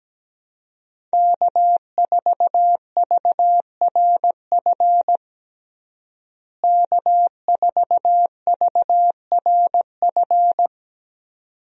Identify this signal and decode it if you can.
Morse code, heard as a keyed tone, about 17 wpm, 700 Hz. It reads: K4VRF K4VRF
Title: USA Call Signs - 5 Characters - Repeated 2x - 17wpm